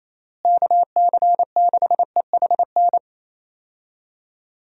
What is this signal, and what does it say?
Morse code, heard as a keyed tone, about 28 wpm, 700 Hz. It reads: KC6EHN
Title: USA Call Signs - 6 Characters - Repeated 3x - 28wpm